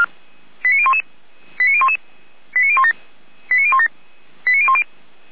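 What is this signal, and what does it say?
Five-tone selective call (ZVEI), heard as quick runs of five tones, tones of about 70 ms. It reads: 89011 89011 89017 89017 89010
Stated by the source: ZVEI five-tone selective call received by SDR.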